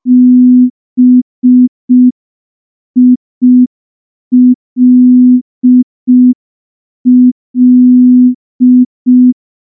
Morse code, heard as a keyed tone, about 5 wpm, 250 Hz. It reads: BILL
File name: morse / morse-training-05WPM